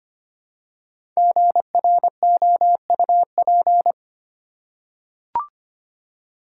Morse code, heard as a keyed tone, about 25 wpm, 700 Hz. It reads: GROUP
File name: 5 Letter Words - First 3 Letters Given - 25wpm